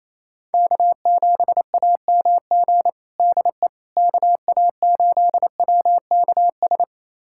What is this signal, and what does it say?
Morse code, heard as a keyed tone, about 28 wpm, 700 Hz. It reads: K7AMG DE KA8WKS
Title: USA Callsign DE Callsign 28wpm